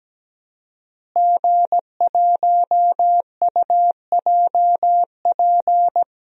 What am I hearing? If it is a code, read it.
Morse code, heard as a keyed tone, about 17 wpm, 700 Hz. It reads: G1UJP
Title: DXCC-100 Call Signs - 5 Characters - Repeated 3x - 17wpm